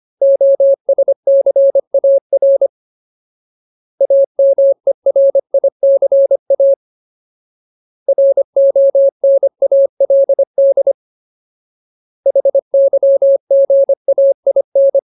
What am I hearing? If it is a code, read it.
Morse code, heard as a keyed tone, about 25 wpm, 550 Hz. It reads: OSCAR AMERICA RONALD HYGAIN
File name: Ultimate Word List / 25WPM Ultimate